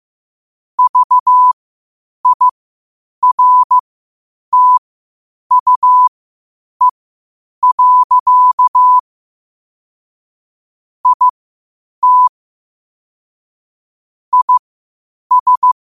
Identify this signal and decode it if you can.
Morse code, heard as a keyed tone, about 15 wpm, spaced as for 8 wpm, 1000 Hz. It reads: VIRTUE. IT IS
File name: qotdfiles / 8wpm